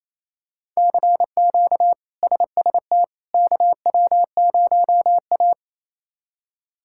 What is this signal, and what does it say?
Morse code, heard as a keyed tone, about 28 wpm, 700 Hz. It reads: CQ SST KW0A
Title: K1USN SST QSO Practice - 28wpm